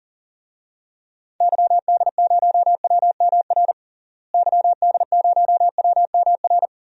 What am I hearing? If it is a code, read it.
Morse code, heard as a keyed tone, about 40 wpm, 700 Hz. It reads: YD0WMR YD0WMR
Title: DXCC-100 Call Signs - 6 Characters - Repeated 3x - 40wpm